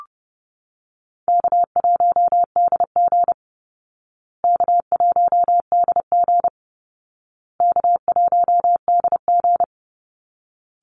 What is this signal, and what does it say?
Morse code, heard as a keyed tone, about 30 wpm, 700 Hz. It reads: K1DG K1DG K1DG
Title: Call Signs Repeated 2x - 30wpm